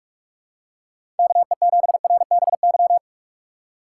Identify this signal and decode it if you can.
Morse code, heard as a keyed tone, about 47 wpm, 700 Hz. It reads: KE7RBY
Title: USA Call Signs - 6 Characters - 3x Character Spacing - 45wpm